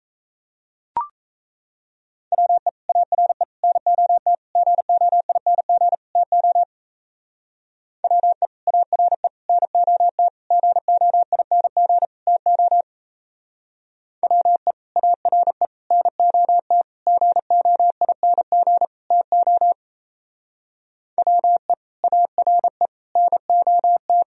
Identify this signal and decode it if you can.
Morse code, heard as a keyed tone, about 36 wpm, 700 Hz. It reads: WE ARE NOT GOING TO WE ARE NOT GOING TO WE ARE NOT GOING TO WE ARE NOT